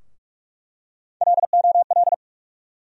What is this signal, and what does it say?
Morse code, heard as a keyed tone, about 46 wpm, 700 Hz. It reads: LOL
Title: USA Call Sign Postfixes - 3 Letters 45wpm